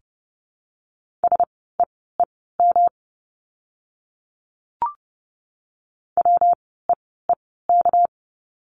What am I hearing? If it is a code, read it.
Morse code, heard as a keyed tone, about 30 wpm, spaced as for 17 wpm, 700 Hz. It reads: SEEM WEEK
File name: course / 30:10wpm / 230 Lesson 37f - Letter x - Words 30wpm